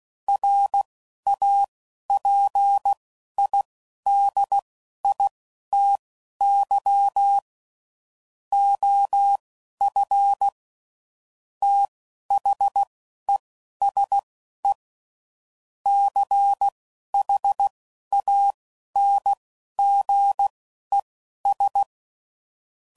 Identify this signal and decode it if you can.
Morse code, heard as a keyed tone, about 16 wpm, spaced as for 11 wpm, 800 Hz. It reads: RAPIDITY OF THESE CHANGES
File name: CW Code Groups @10WPM / Technical Subjects / Reactance